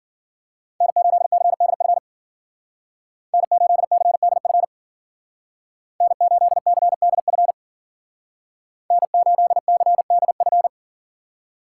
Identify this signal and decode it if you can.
Morse code, heard as a keyed tone, about 50 wpm, 700 Hz. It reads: N8CDF N8CDF N8CDF N8CDF
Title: Speed Racing - USA 1x3 Call Signs 40wpm